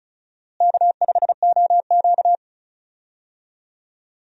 Morse code, heard as a keyed tone, about 36 wpm, 700 Hz. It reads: K5OQ
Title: USA Call Signs - 4 Characters - Repeated 2x - 35wpm